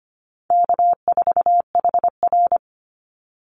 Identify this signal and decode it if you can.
Morse code, heard as a keyed tone, about 25 wpm, 700 Hz. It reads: K4HR
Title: Easy Call Signs Repeated 3x - 25wpm